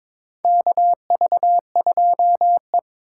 Morse code, heard as a keyed tone, about 22 wpm, 700 Hz. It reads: KV2E
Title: USA Call Signs - 3x Character Spacing - 22wpm